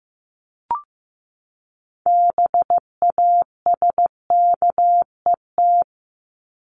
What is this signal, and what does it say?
Morse code, heard as a keyed tone, about 15 wpm, 700 Hz. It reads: BASKET
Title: Top 1751-2000 Words 15wpm